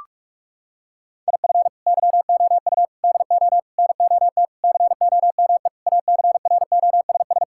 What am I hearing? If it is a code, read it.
Morse code, heard as a keyed tone, about 46 wpm, 700 Hz. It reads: IF YOU DO NOT COME ACROSS
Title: Sentences from Top 1000 Words - Review Part 5 - 45wpm